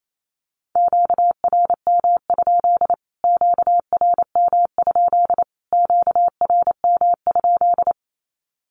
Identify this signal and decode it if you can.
Morse code, heard as a keyed tone, about 28 wpm, 700 Hz. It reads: QRM? QRM? QRM?
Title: Q-Codes Repeated 3x 28wpm